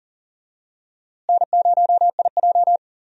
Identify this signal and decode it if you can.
Morse code, heard as a keyed tone, about 40 wpm, 700 Hz. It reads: N0IJ
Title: USA Call Signs - 4 Characters - Repeated 3x - 40wpm